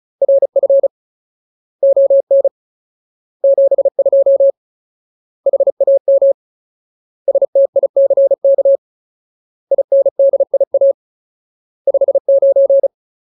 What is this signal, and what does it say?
Morse code, heard as a keyed tone, about 34 wpm, 550 Hz. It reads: RF ON 72 HAM STICK INDIA 59
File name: Ultimate Word List / 35WPM Ultimate